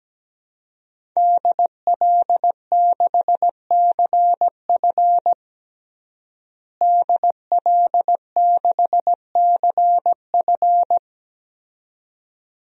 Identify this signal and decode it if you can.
Morse code, heard as a keyed tone, about 17 wpm, 700 Hz. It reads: DL6CF DL6CF
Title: DXCC-100 Call Signs - 5 Characters - Repeated 2x - 17wpm